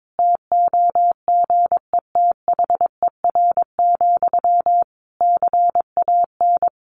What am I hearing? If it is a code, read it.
Morse code, heard as a keyed tone, about 22 wpm, 700 Hz. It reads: TOGETHER, CAN
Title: Sentences from Top 1400 Words 22wpm